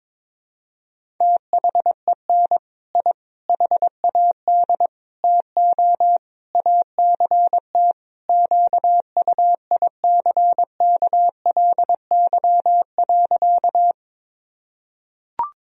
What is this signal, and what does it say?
Morse code, heard as a keyed tone, about 22 wpm, 700 Hz. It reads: THEN I HAD TO ACT QUICKLY.
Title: Sentences from Top 1000 Words - Review Part 1 - 22wpm